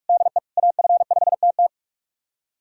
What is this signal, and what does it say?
Morse code, heard as a keyed tone, about 46 wpm, 700 Hz. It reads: DE AF5TT
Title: USA Callsign DE Callsign - 5 Characters - 45wpm